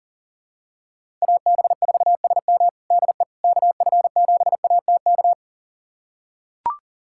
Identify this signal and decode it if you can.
Morse code, heard as a keyed tone, about 40 wpm, 700 Hz. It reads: AB4SM DE KF7ATK
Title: USA Callsign DE Callsign 40wpm